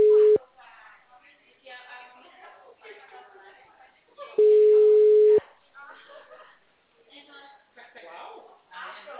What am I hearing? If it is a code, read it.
Ringback tone (Europe, 425 Hz), with speech.
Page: Atibaia - SP 1 (CP Cartao)